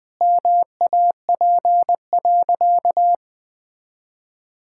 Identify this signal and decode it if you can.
Morse code, heard as a keyed tone, about 21 wpm, 700 Hz. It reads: MAP.
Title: Sentences with 5-Word Phrases - 20wpm